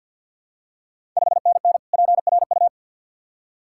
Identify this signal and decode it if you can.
Morse code, heard as a keyed tone, about 52 wpm, 690 Hz. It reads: 5NN PRU
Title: Ontario QSO Party Exchange 50wpm